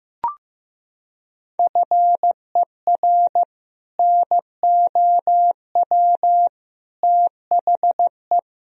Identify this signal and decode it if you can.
Morse code, heard as a keyed tone, about 15 wpm, 700 Hz. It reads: FER NOW THE
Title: Sentences from Top 200 Words and Abbreviations 15wpm